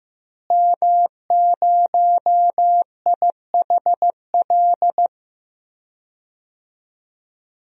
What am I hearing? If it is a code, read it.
Morse code, heard as a keyed tone, about 15 wpm, 700 Hz. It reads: M0IHL
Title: DXCC-100 Call Signs - 5 Characters - Repeated 3x - 15wpm